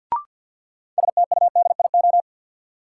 Morse code, heard as a keyed tone, about 52 wpm, 700 Hz. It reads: STUDIO
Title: Sentences from Top 1200 Words - New Word Drill Alt - 50wpm